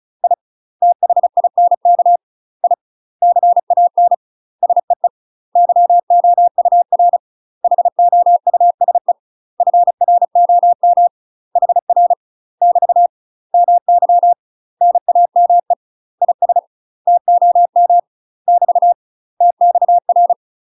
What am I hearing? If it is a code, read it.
Morse code, heard as a keyed tone, about 34 wpm, 700 Hz. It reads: I THINK I CAN SEE YOUR HOUSE FROM HR <BT> MY NAME IS TOM <BT> TXR
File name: UK MASTER QSOs TRAINING / 35WPM / UK MASTER QSOs 0040